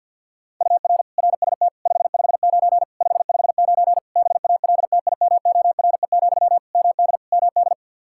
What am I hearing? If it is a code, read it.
Morse code, heard as a keyed tone, about 52 wpm, 700 Hz. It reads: UR RST 559 559 BALTIMORE, MD MD
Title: Practice QSO - Exchange 1 - Beginning - 50wpm